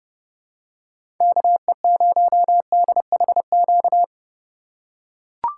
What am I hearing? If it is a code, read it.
Morse code, heard as a keyed tone, about 30 wpm, 700 Hz. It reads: KE0DHQ
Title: USA Call Signs - 6 Characters - 4x Character Spacing - 30wpm